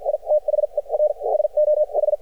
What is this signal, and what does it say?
Morse code, heard as a keyed tone, about 52 wpm, spaced as for 43 wpm, 600 Hz. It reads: I THE R SO H
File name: archived / CRSnet